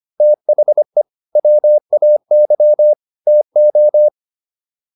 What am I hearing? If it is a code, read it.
Morse code, heard as a keyed tone, about 25 wpm, 600 Hz. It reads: THE WAY TO
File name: WD205 25